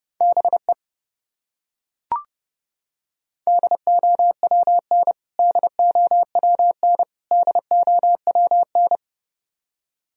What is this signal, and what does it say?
Morse code, heard as a keyed tone, about 30 wpm, 700 Hz. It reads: BE DOWN DOWN DOWN
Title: Top 100 Words Repeated 30wpm